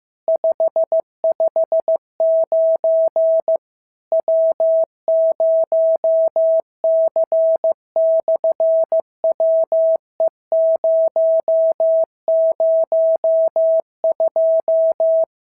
Morse code, heard as a keyed tone, about 15 wpm, 650 Hz. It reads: 559 W0C/WE002
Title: SOTA QSO Practice 15wpm